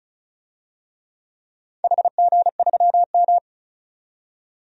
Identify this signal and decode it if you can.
Morse code, heard as a keyed tone, about 36 wpm, 700 Hz. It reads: HG3M
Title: DXCC-100 Call Signs - 4 Characters - 35wpm